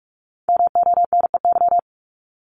Morse code, heard as a keyed tone, about 46 wpm, 700 Hz. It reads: MONEY
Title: Multi-letter Phonemes Repeated 3x 45wpm